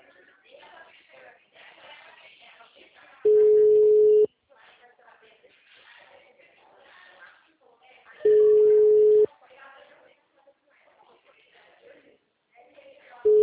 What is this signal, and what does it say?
Ringback tone (Europe, 425 Hz), with speech.
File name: out- -2009- - - .334